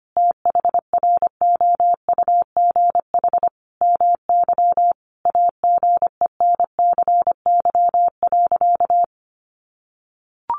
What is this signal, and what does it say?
Morse code, heard as a keyed tone, about 25 wpm, 700 Hz. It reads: THROUGH MY AGENCY.
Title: Sentences from Top 1300 Words 25wpm